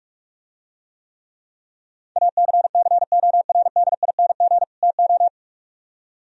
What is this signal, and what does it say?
Morse code, heard as a keyed tone, about 46 wpm, 700 Hz. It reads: ACCORDING TO